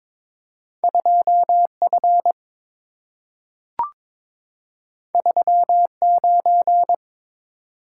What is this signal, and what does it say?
Morse code, heard as a keyed tone, about 22 wpm, 700 Hz. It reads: 2F 39
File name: practice / 22wpm / Two Letter-Number Combinations - 22wpm